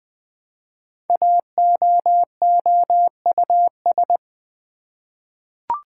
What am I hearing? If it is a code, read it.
Morse code, heard as a keyed tone, about 20 wpm, 700 Hz. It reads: AOOUS
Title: Code Groups - 5 Letters - 20wpm